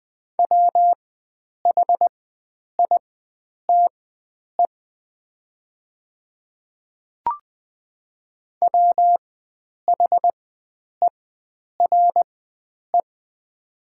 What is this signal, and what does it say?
Morse code, heard as a keyed tone, about 20 wpm, spaced as for 9 wpm, 700 Hz. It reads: WHITE WHERE
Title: 4x Character Spacing - Top 500 Words - 20wpm